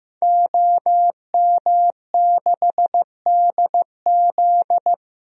Morse code, heard as a keyed tone, about 15 wpm, 700 Hz. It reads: OM6DZ
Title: DXCC-100 Call Signs - 5 Characters - 15wpm